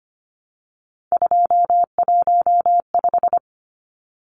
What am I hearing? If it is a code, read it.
Morse code, heard as a keyed tone, about 25 wpm, 690 Hz. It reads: 215
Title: Numbers - 3-Digit 25wpm